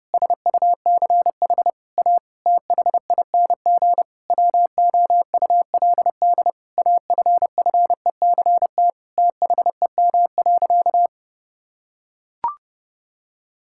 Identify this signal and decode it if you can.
Morse code, heard as a keyed tone, about 30 wpm, 700 Hz. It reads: SUCH A THING WOULD AFFECT THEM.
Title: Sentences from Top 1000 Words - Review Part 6 - 30wpm